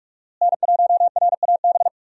Morse code, heard as a keyed tone, about 42 wpm, 700 Hz. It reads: N1RAB
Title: USA Call Signs - 5 Characters - Repeated 2x - 45wpm